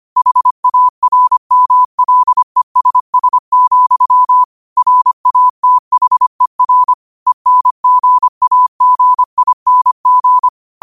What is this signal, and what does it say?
Morse code, heard as a keyed tone, about 25 wpm, 1000 Hz. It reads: HARMLESS, RATHER ENGAGING
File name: qotdfiles / 25wpm